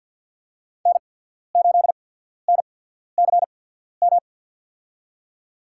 Morse code, heard as a keyed tone, about 49 wpm, spaced as for 14 wpm, 700 Hz. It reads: N8NCM
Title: USA Call Signs - 5 Characters - 8x Character Spacing - 50wpm